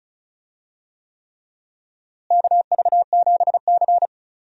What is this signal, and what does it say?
Morse code, heard as a keyed tone, about 36 wpm, 700 Hz. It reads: KV7C
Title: USA Call Signs - 4 Characters - Repeated 3x - 35wpm